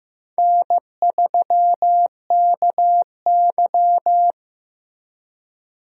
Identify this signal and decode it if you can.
Morse code, heard as a keyed tone, about 15 wpm, 700 Hz. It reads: N3KY
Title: USA Call Signs - 4 Characters - Repeated 2x - 15wpm